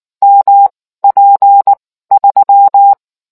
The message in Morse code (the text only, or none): MP3